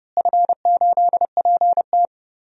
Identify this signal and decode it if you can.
Morse code, heard as a keyed tone, about 30 wpm, 700 Hz. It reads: F8PT
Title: DXCC-100 Call Signs - 4 Characters - Repeated 3x - 30wpm